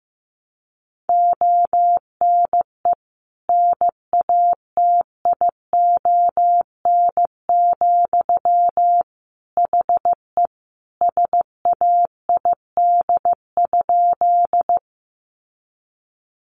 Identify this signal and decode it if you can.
Morse code, heard as a keyed tone, about 15 wpm, 700 Hz. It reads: ONE NATION, HE SAID?
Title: Sentences from Top 900 Words 15wpm